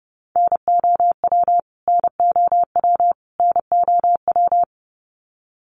Morse code, heard as a keyed tone, about 30 wpm, 700 Hz. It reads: NOW NOW NOW
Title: Top 300 Words Repeated 30wpm